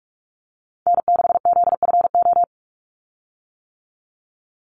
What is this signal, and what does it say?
Morse code, heard as a keyed tone, about 44 wpm, 700 Hz. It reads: N6ZFO
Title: Call Signs - 45wpm